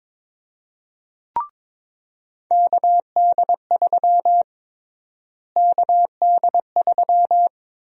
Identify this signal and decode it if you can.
Morse code, heard as a keyed tone, about 22 wpm, 700 Hz. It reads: KD3 KD3
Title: USA Call Sign Prefixes - Repeated 2x - 22wpm